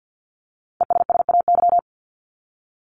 Morse code, heard as a keyed tone, about 50 wpm, 700 Hz. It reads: ESSAY
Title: Top 1251-1500 Words 50wpm